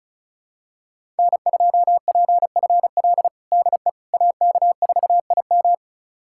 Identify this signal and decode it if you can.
Morse code, heard as a keyed tone, about 36 wpm, 700 Hz. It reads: N2PFL DE AK4IM
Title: USA Callsign DE Callsign - 5 Characters - 35wpm